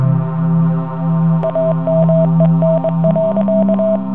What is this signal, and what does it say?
Morse code, heard as a keyed tone, about 23 wpm, 650 Hz. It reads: AMEN.